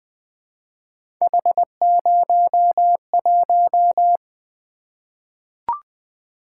Morse code, heard as a keyed tone, about 21 wpm, 700 Hz. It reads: H01
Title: Three Letter-Number Combinations - 20wpm